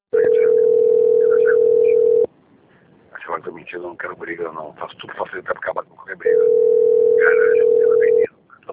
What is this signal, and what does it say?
Ringback tone (North America), 440 + 480 Hz, with speech.
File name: out- - - - - .25356